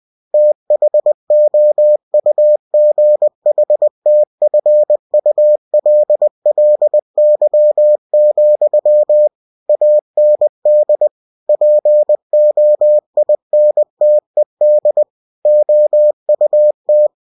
Morse code, heard as a keyed tone, about 20 wpm, 600 Hz. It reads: THOUGHTFULLY, AND POINTED OUT
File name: cw-books / winnie-the-pooh / winnie-20wpm-page-0018